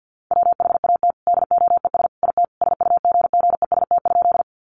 Morse code, heard as a keyed tone, about 50 wpm, 700 Hz. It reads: WHAT DOES IT SUGGEST?
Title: Sentences from Top 1300 Words 50wpm